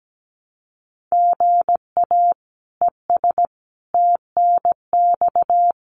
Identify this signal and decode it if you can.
Morse code, heard as a keyed tone, about 17 wpm, 700 Hz. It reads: GA ES TNX
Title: Common QSO Phrases Encore 17wpm